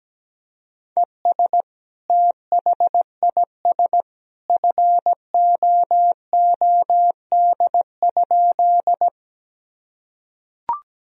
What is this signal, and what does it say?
Morse code, heard as a keyed tone, about 17 wpm, 700 Hz. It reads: ES THIS FOOD?
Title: Sentences from Top 300 Words and Abbreviations 17wpm